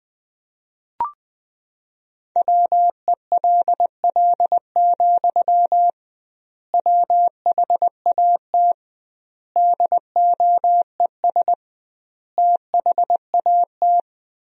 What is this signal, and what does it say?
Morse code, heard as a keyed tone, about 20 wpm, 700 Hz. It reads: WELL, WHAT DOES THAT
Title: 2x Word-Spacing - Sentences from Top 500 Words - 20wpm